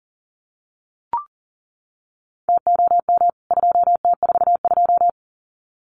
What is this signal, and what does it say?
Morse code, heard as a keyed tone, about 40 wpm, 700 Hz. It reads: TOM 2T42
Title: CWT Exchange 40wpm